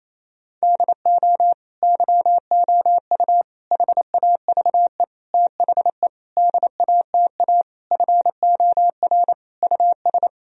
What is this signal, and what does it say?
Morse code, heard as a keyed tone, about 28 wpm, 700 Hz. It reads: DO YOU HAVE THE DATA FOR US
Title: Sentences from Top 1000 Words - Review Part 3 - 28wpm